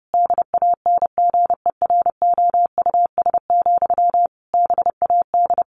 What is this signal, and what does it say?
Morse code, heard as a keyed tone, about 30 wpm, 700 Hz. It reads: DANGEROUS, BAD
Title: Sentences from Top 1200 Words 30wpm